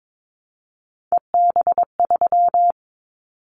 Morse code, heard as a keyed tone, about 22 wpm, 700 Hz. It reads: EB3
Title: Callsign Prefixes for Top 25 DXCC - 22wpm